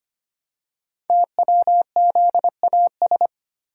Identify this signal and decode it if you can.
Morse code, heard as a keyed tone, about 25 wpm, 700 Hz. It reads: TWZAS
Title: Code Groups - 5 Letters - 25wpm